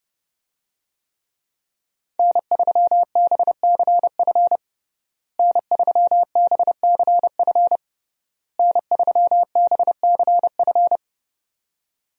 Morse code, heard as a keyed tone, about 32 wpm, 700 Hz. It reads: N3BCF N3BCF N3BCF
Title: USA Call Signs - 5 Characters - Repeated 3x - 30wpm